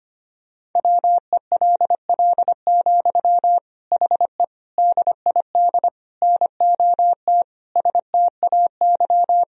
Morse code, heard as a keyed tone, about 25 wpm, 700 Hz. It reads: WELL, HE DID NOT STAY
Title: Sentences from Top 1000 Words - Review Part 2 - 25wpm